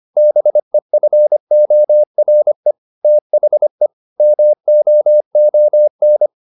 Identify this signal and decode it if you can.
Morse code, heard as a keyed tone, about 25 wpm, 600 Hz. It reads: BEFORE THE MOON